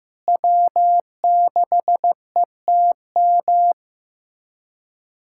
Morse code, heard as a keyed tone, about 15 wpm, 700 Hz. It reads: W6ETM
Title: USA Callsign DE Callsign - 5 Characters - 15wpm